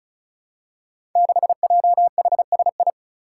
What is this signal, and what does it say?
Morse code, heard as a keyed tone, about 36 wpm, 700 Hz. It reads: 6JHSI
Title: Five Challenging Letter-Number Combinations 35wpm